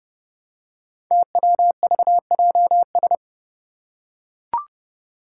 Morse code, heard as a keyed tone, about 28 wpm, 700 Hz. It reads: TWVJS
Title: Five Challenging Letter-Number Combinations 30wpm